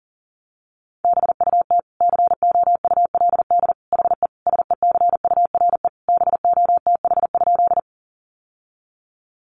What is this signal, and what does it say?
Morse code, heard as a keyed tone, about 37 wpm, 700 Hz. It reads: BUT COULD HE SECURE BOTH?
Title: Sentences from Top 1200 Words 40wpm